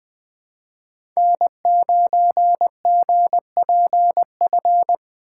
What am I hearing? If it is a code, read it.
Morse code, heard as a keyed tone, about 20 wpm, 700 Hz. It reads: N9GPF
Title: USA Call Signs - 5 Characters - 2x Character Spacing - 20wpm